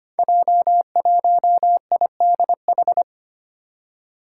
Morse code, heard as a keyed tone, about 25 wpm, 700 Hz. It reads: J1IDH
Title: Five Challenging Letter-Number Combinations 25wpm